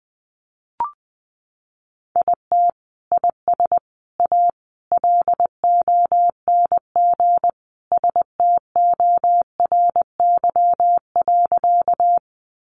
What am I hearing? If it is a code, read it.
Morse code, heard as a keyed tone, about 20 wpm, 700 Hz. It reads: IT IS A LONG STORY.
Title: Sentences from Top 300 Words 20wpm